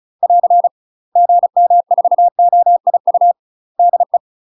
Morse code, heard as a keyed tone, about 34 wpm, 700 Hz. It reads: <AR> GM4OIU DE
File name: UK MASTER QSOs TRAINING / 35WPM / UK MASTER QSOs 0043